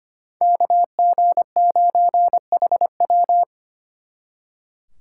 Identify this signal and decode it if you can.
Morse code, heard as a keyed tone, about 31 wpm, 700 Hz. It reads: KG9HW
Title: Speed Racing - USA 2x2 Call Signs 25wpm